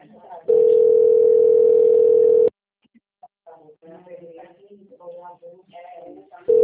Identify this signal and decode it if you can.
Ringback tone (North America), 440 + 480 Hz, with speech.
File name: out- - - - - .9541